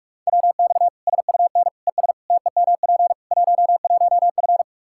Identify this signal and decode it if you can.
Morse code, heard as a keyed tone, about 46 wpm, 700 Hz. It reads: WX SUN ES TEMP 11F
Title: Practice QSO - Exchange 2 - Info - 45wpm